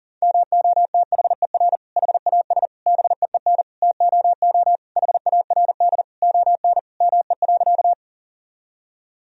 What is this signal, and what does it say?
Morse code, heard as a keyed tone, about 40 wpm, 700 Hz. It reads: MOTHER HAS BEEN TOO HARD ON ME.
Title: Sentences from Top 1000 Words - Review Part 1 - 40wpm